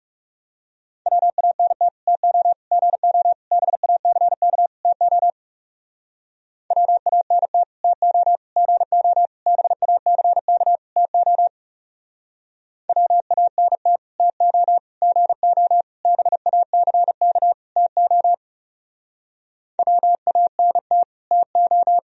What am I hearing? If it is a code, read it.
Morse code, heard as a keyed tone, about 37 wpm, 700 Hz. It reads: WANT TO GO BACK TO WANT TO GO BACK TO WANT TO GO BACK TO WANT TO
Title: Speed-Racing - Sets of 5 Words 30wpm